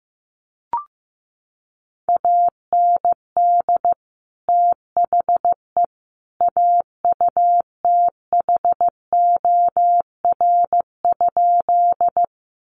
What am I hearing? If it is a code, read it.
Morse code, heard as a keyed tone, about 15 wpm, 700 Hz. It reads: AND THE AUTHOR?
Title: Sentences from Top 1200 Words 15wpm